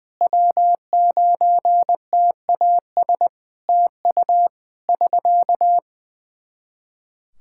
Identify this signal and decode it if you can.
Morse code, heard as a keyed tone, about 20 wpm, 700 Hz. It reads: W9TAS TU <SK>
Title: Practice QSO - Exchange 4 - Ending - 20wpm